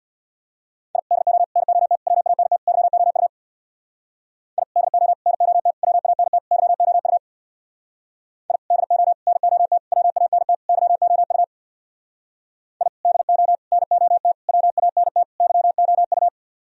Morse code, heard as a keyed tone, about 62 wpm, 700 Hz. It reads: I DO NOT WANT YOU I DO NOT WANT YOU I DO NOT WANT YOU I DO NOT WANT YOU
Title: Speed-Racing - Sets of 5 Words 50wpm